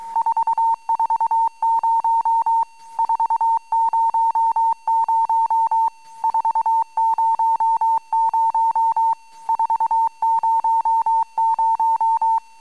Morse code, heard as a keyed tone, about 23 wpm, 910 Hz. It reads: 440 400 400 400
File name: amsat / cw